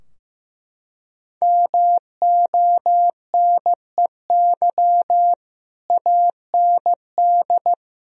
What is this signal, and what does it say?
Morse code, heard as a keyed tone, about 15 wpm, 700 Hz. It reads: MONEY AND
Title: Sentences from Top 1000 Words - Review Part 4 - 15wpm